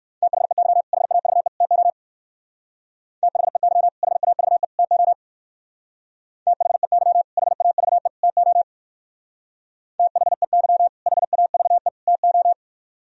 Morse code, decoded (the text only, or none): THEY HAVE TO THEY HAVE TO THEY HAVE TO THEY HAVE TO